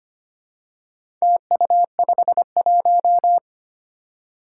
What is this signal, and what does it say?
Morse code, heard as a keyed tone, about 25 wpm, 700 Hz. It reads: TU51
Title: Four Challenging Letter-Number Combinations 25wpm